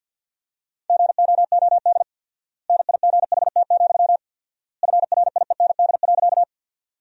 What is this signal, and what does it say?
Morse code, heard as a keyed tone, about 50 wpm, 700 Hz. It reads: GOOD NIGHT, FRIEND.
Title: 4x Word-Spacing - Sentences from Top 300 Words - 50wpm